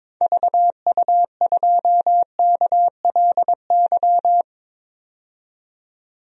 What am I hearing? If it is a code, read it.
Morse code, heard as a keyed tone, about 22 wpm, 700 Hz. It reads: VU2KLY
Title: DXCC-100 Call Signs - 6 Characters - 22wpm